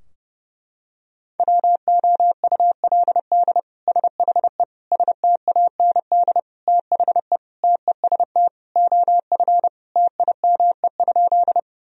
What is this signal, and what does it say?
Morse code, heard as a keyed tone, about 30 wpm, 700 Hz. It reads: WOULD SHE STAND THE TEST OF TIME?
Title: Sentences from Top 1000 Words - Review Part 3 - 30wpm